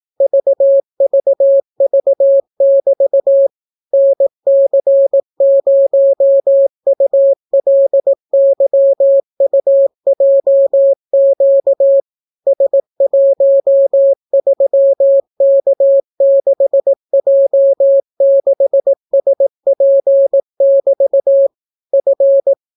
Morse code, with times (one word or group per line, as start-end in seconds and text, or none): 0.2-3.5 VVV<BT>
3.9-12.0 NC0ULYUJQ
12.5-21.5 S13K6J6SP<BT>
21.9-22.5 F